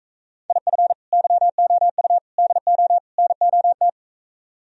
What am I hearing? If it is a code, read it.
Morse code, heard as a keyed tone, about 45 wpm, 700 Hz. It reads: IF YOU DO NOT